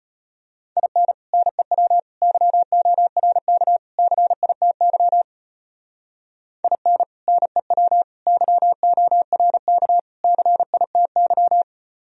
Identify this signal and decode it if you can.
Morse code, heard as a keyed tone, about 30 wpm, 700 Hz. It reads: IN NEW YORK CITY IN NEW YORK CITY
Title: Speed-Racing - Sets of 4 Words 25wpm